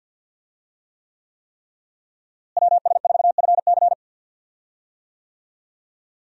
Morse code, heard as a keyed tone, about 48 wpm, 700 Hz. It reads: WS4FC
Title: USA Call Signs - 5 Characters - Repeated 3x - 50wpm